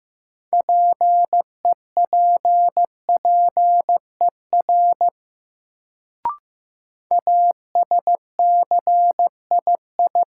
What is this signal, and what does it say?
Morse code, heard as a keyed tone, about 15 wpm, 700 Hz. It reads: PEPPER ASCII
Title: Sets of 5 Words - New Word Drill - 15wpm